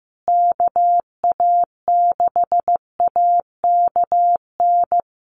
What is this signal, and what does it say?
Morse code, heard as a keyed tone, about 14 wpm, 700 Hz. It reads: KA6AKN
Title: USA Call Signs Repeated 3x 15wpm